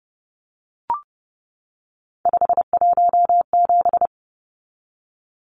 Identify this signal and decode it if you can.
Morse code, heard as a keyed tone, about 30 wpm, 700 Hz. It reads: 517
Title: Numbers - 3-Digit 30wpm